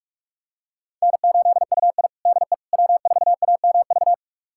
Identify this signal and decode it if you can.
Morse code, heard as a keyed tone, about 41 wpm, 700 Hz. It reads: N8UI DE W4AMV
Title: USA Callsign DE Callsign 45wpm